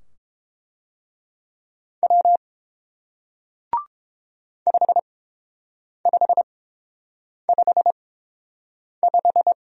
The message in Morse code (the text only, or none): W 5555